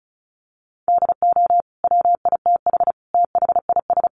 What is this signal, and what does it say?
Morse code, heard as a keyed tone, about 34 wpm, 700 Hz. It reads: DO WITH THIS
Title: Sentences from Top 1400 Words 35wpm